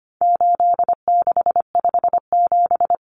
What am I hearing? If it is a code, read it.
Morse code, heard as a keyed tone, about 25 wpm, 700 Hz. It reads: 8657
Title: Common Cut Numbers - 4-Digit 25wpm